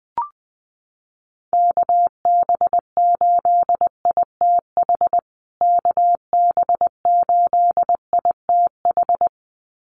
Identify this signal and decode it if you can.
Morse code, heard as a keyed tone, about 20 wpm, 700 Hz. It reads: KB8ITH KB8ITH
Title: USA Call Signs Repeated 2x 20wpm